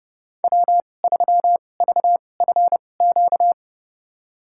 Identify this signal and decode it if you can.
Morse code, heard as a keyed tone, about 32 wpm, spaced as for 22 wpm, 700 Hz. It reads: W3VFQ
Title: USA Call Signs - 5 Characters - 2x Character Spacing - 30wpm